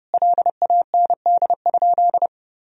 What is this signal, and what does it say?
Morse code, heard as a keyed tone, about 30 wpm, 700 Hz. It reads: LAND?
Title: Sentences from Top 1000 Words - Review Part 2 - 30wpm